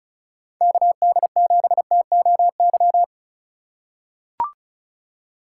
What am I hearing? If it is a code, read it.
Morse code, heard as a keyed tone, about 36 wpm, 700 Hz. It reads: KD7TOY
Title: USA Call Signs - 6 Characters - 3x Character Spacing - 35wpm